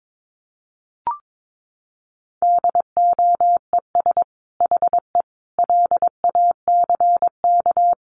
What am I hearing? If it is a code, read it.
Morse code, heard as a keyed tone, about 22 wpm, 700 Hz. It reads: DOES HE LACK
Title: Sentences from Top 1100 Words 22wpm